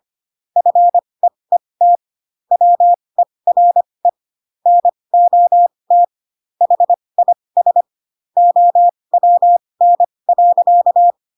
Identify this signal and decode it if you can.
Morse code, heard as a keyed tone, about 25 wpm, spaced as for 20 wpm, 700 Hz. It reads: FEET WERE NOT HIS OWN.
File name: Pinocchio / Pinocchio ch6 15 25 ik1hge